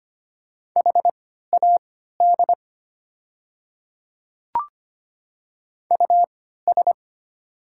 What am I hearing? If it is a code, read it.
Morse code, heard as a keyed tone, about 25 wpm, spaced as for 14 wpm, 690 Hz. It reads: HAD US
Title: 3x Character Spacing - Top 200 Words - 25wpm